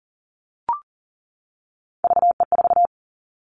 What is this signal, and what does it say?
Morse code, heard as a keyed tone, about 40 wpm, 700 Hz. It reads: VE4
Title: Callsign Prefixes for Top 100 DXCC - 40wpm